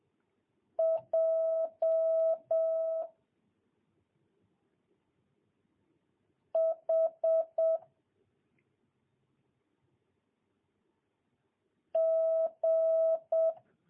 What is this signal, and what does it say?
Morse code, heard as a keyed tone, about 7 wpm, 650 Hz. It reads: J H G
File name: jh8cbh / 2022cw